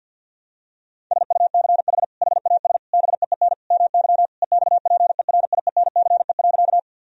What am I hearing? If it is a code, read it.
Morse code, heard as a keyed tone, about 50 wpm, 700 Hz. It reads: SUCH HAS BEEN MY EXPERIENCE.
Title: Sentences from Top 1000 Words - Review Part 3 - 50wpm